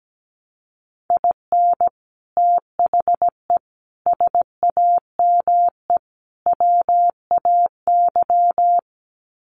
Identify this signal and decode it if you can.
Morse code, heard as a keyed tone, about 17 wpm, 700 Hz. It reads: IN THE SAME WAY
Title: Sets of 4 Words 17wpm